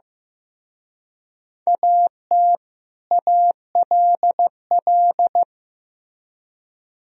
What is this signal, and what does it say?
Morse code, heard as a keyed tone, about 18 wpm, 700 Hz. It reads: AT ALL
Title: Speed-Racing - Sets of 2 Words 15wpm